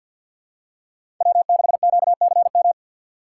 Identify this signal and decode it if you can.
Morse code, heard as a keyed tone, about 53 wpm, 700 Hz. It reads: W6QCM
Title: USA Call Signs - 5 Characters - 3x Character Spacing - 50wpm